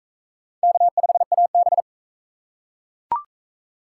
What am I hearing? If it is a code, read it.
Morse code, heard as a keyed tone, about 42 wpm, 700 Hz. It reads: K5AB